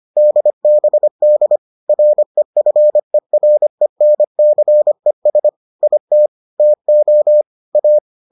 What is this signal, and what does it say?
Morse code, heard as a keyed tone, about 25 wpm, 600 Hz. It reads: DBD REFERENCES IT TO A